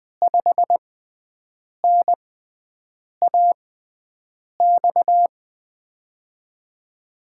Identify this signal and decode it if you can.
Morse code, heard as a keyed tone, about 20 wpm, spaced as for 7 wpm, 700 Hz. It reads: 5NAX
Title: USA Call Signs - 6 Characters - 6x Character Spacing - 20wpm